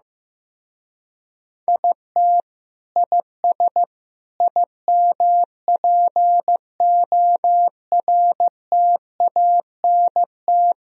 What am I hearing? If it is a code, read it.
Morse code, heard as a keyed tone, about 16 wpm, 700 Hz. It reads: IT IS IMPORTANT
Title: Speed-Racing - Sets of 4 Words 15wpm